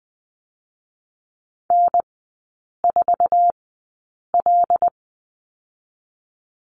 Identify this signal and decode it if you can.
Morse code, heard as a keyed tone, about 20 wpm, 700 Hz. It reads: N 4 L
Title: 100 Lesson 17e - Letter u - Three Characters 20wpm